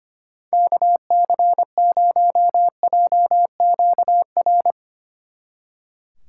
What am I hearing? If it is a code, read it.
Morse code, heard as a keyed tone, about 33 wpm, 700 Hz. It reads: KC0JQR